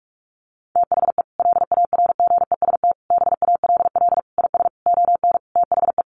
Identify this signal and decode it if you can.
Morse code, heard as a keyed tone, about 44 wpm, 700 Hz. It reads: THE LARGEST BALL IS ON THE
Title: Easy Phrases 45wpm